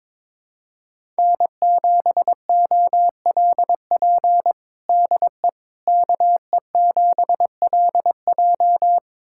Morse code, heard as a keyed tone, about 21 wpm, 700 Hz. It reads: N7OLP DE KE7LJ
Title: USA Callsign DE Callsign - 5 Characters - 22wpm